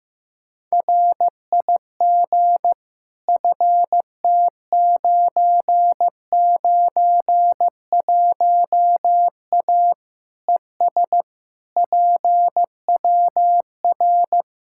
Morse code, heard as a keyed tone, about 15 wpm, 700 Hz. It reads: RIG FT991A ES PWR
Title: Practice QSO - Exchange 2 - Info - 15wpm